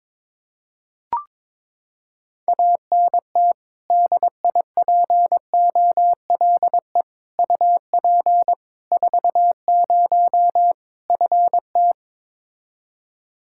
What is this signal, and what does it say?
Morse code, heard as a keyed tone, about 22 wpm, 700 Hz. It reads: ANT DIPOLE UP 40 FT
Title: Practice QSO - Exchange 2 - Info - 22wpm